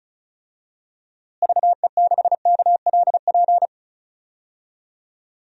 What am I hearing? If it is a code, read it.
Morse code, heard as a keyed tone, about 34 wpm, 700 Hz. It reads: VE6KLP
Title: DXCC-100 Call Signs - 6 Characters - Repeated 2x - 35wpm